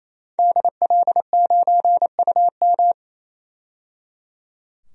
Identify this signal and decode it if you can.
Morse code, heard as a keyed tone, about 28 wpm, 700 Hz. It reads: DL9UM
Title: DXCC-100 Call Signs - 5 Characters - 28wpm